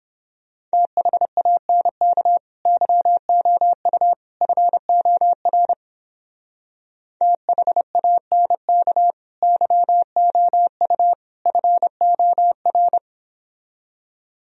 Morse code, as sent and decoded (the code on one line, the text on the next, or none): - .... .- -. -.- / -.-- --- ..- / ..-. --- .-. / - .... .- -. -.- / -.-- --- ..- / ..-. --- .-.
THANK YOU FOR THANK YOU FOR